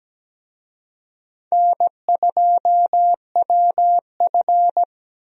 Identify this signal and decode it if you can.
Morse code, heard as a keyed tone, about 17 wpm, 700 Hz. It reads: N2WF
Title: USA Call Signs - 4 Characters - Repeated 2x - 17wpm